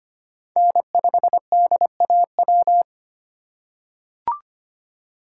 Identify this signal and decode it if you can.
Morse code, heard as a keyed tone, about 25 wpm, 700 Hz. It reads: N5DAW
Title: USA Call Signs - 5 Characters - 3x Character Spacing - 25wpm